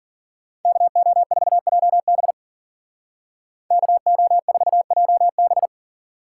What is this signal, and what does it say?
Morse code, heard as a keyed tone, about 50 wpm, 700 Hz. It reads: KO4JB KO4JB